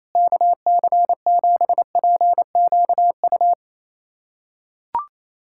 Morse code, heard as a keyed tone, about 28 wpm, 700 Hz. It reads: KC7PQU
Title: USA Callsign DE Callsign - 6 Characters - 28wpm